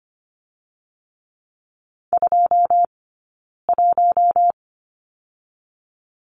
Morse code, heard as a keyed tone, about 25 wpm, 700 Hz. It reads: 2 1
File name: Lesson 15d - Number 5 - Two Characters 25wpm